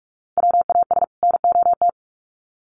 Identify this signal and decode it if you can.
Morse code, heard as a keyed tone, about 44 wpm, 700 Hz. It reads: WAS NOT
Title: Sets of 2 Words 45wpm